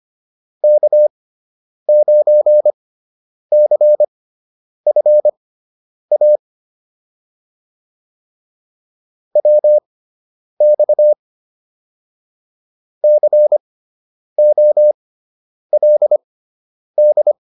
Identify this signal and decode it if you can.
Morse code, heard as a keyed tone, about 25 wpm, spaced as for 9 wpm, 600 Hz. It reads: K9CFA WX COLD
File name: QSO 202 09